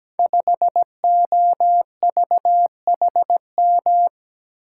Morse code, heard as a keyed tone, about 17 wpm, 700 Hz. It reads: 5OVHM
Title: Code Groups - 5 Letter:Numbers - 17wpm